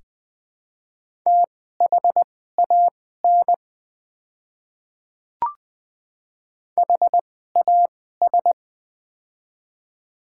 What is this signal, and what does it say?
Morse code, heard as a keyed tone, about 21 wpm, spaced as for 15 wpm, 700 Hz. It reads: THAN HAS
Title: 2x Character Spacing - Top 200 Words - 20wpm